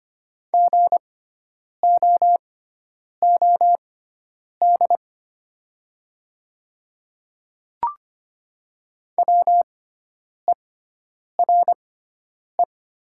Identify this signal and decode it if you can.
Morse code, heard as a keyed tone, about 25 wpm, spaced as for 9 wpm, 700 Hz. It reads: GOOD WERE
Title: 6x Character Spacing - Top 200 Words - 25wpm